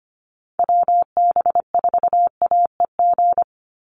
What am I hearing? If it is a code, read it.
Morse code, heard as a keyed tone, about 25 wpm, 700 Hz. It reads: WB4AEG
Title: Call Signs - 25wpm